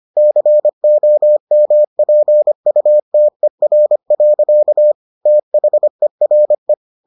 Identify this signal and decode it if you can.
Morse code, heard as a keyed tone, about 25 wpm, 600 Hz. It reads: COMPUTER. THERE